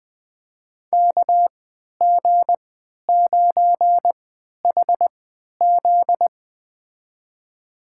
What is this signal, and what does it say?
Morse code, heard as a keyed tone, about 20 wpm, spaced as for 11 wpm, 700 Hz. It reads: KG9HZ
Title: USA Call Signs - 5 Characters - 3x Character Spacing - 20wpm